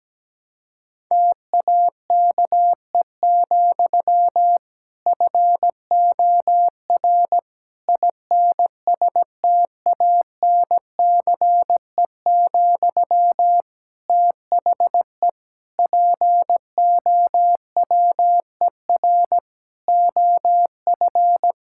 Morse code, heard as a keyed tone, about 17 wpm, 700 Hz. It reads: TAKE, FOR INSTANCE, THE POWER OF
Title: Sentences from Top 1000 Words - Review Part 5 - 17wpm